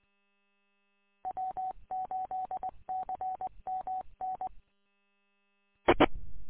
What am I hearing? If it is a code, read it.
Morse code, heard as a keyed tone, about 20 wpm, 750 Hz. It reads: W8CMN I
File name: w8cmn- -combined